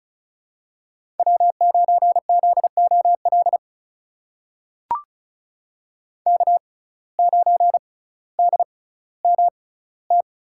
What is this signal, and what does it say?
Morse code, heard as a keyed tone, about 34 wpm, spaced as for 18 wpm, 700 Hz. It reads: W9ZOL K9DMT
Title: USA Call Signs - 5 Characters - 6x Character Spacing - 35wpm